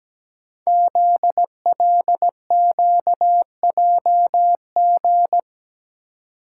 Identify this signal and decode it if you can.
Morse code, heard as a keyed tone, about 17 wpm, 700 Hz. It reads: ZLQJG
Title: Code Groups - 5 Letters - 17wpm